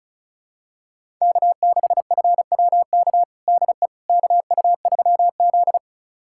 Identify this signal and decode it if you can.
Morse code, heard as a keyed tone, about 36 wpm, 700 Hz. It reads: K6FWK DE KU3Z
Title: USA Callsign DE Callsign 35wpm